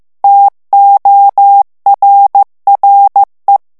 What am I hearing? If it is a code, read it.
Morse code, heard as a keyed tone, about 15 wpm, 800 Hz. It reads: TORRE
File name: morse mp3 / ITALIEN texte clair 1 vitesse 14 mots minute